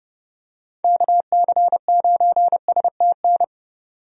Morse code, heard as a keyed tone, about 31 wpm, 700 Hz. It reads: KC9STN
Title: USA Call Signs - 6 Characters - 30wpm